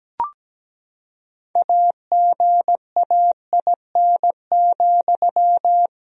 Morse code, heard as a keyed tone, about 17 wpm, 700 Hz. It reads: AGAIN,
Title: Sentences from Top 1000 Words - Review Part 2 - 17wpm